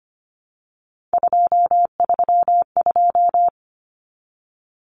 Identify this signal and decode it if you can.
Morse code, heard as a keyed tone, about 25 wpm, 700 Hz. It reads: 232
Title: Numbers - 3-Digit 25wpm